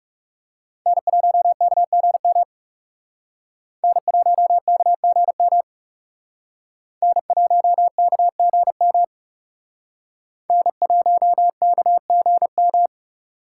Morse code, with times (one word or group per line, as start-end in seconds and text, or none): 0.9-2.4 N1KGM
3.8-5.6 N1KGM
7.0-9.0 N1KGM
10.5-12.9 N1KGM